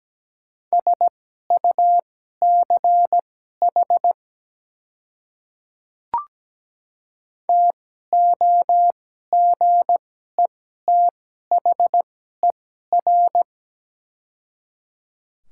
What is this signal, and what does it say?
Morse code, heard as a keyed tone, about 17 wpm, spaced as for 12 wpm, 700 Hz. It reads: SUCH TOGETHER
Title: 2x Character Spacing - Top 300 Words - 17wpm